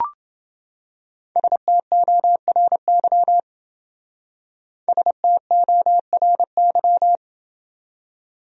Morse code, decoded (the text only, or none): STORY STORY